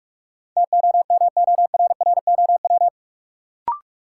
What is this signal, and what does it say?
Morse code, heard as a keyed tone, about 45 wpm, 700 Hz. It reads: TOMORROW
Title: Sentences from Top 1100 Words - New Word Drill - 45wpm